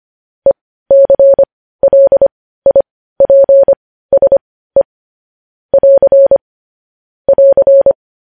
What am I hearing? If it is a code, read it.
Morse code, heard as a keyed tone, about 25 wpm, spaced as for 15 wpm, 550 Hz. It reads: ECLIPSE <AR> <AR>
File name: podcast / News headlines-1x-25x15